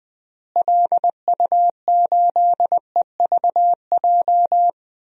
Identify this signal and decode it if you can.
Morse code, heard as a keyed tone, about 20 wpm, 700 Hz. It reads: LU8EVJ
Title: DXCC-100 Call Signs - 6 Characters - Repeated 2x - 20wpm